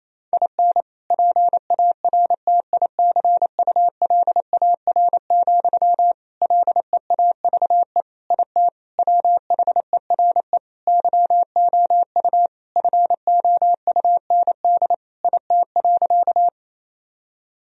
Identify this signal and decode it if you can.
Morse code, heard as a keyed tone, about 27 wpm, 700 Hz. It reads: IN PARTICULAR, LEAVE IT WHERE YOU FOUND IT.
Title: Sentences from Top 1000 Words - Review Part 4 - 28wpm